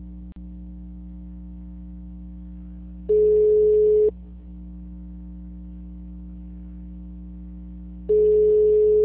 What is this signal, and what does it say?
Ringback tone (Europe, 425 Hz).